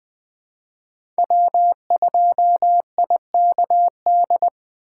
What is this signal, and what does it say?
Morse code, heard as a keyed tone, about 20 wpm, 700 Hz. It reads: W2IKD
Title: USA Call Signs - 5 Characters - 4x Character Spacing - 20wpm